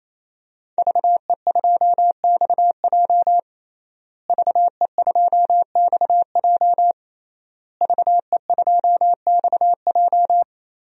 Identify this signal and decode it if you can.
Morse code, heard as a keyed tone, about 28 wpm, 700 Hz. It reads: VE2XJ VE2XJ VE2XJ
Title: DXCC-100 Call Signs - 5 Characters - Repeated 3x - 28wpm